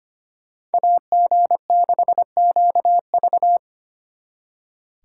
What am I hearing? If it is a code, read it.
Morse code, heard as a keyed tone, about 33 wpm, 700 Hz. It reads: AG6QV